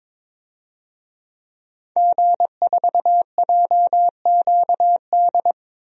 Morse code, heard as a keyed tone, about 21 wpm, 700 Hz. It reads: G4JQD
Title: DXCC-100 Call Signs - 5 Characters - 22wpm